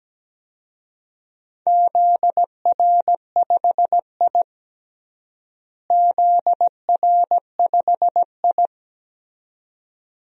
Morse code, heard as a keyed tone, about 17 wpm, 700 Hz. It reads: ZR5I ZR5I
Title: DXCC-100 Call Signs - 4 Characters - Repeated 2x - 17wpm